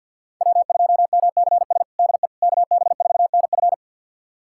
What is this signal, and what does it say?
Morse code, heard as a keyed tone, about 51 wpm, 700 Hz. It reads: W2MCS DE KB4NF
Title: USA Callsign DE Callsign - 5 Characters - 50wpm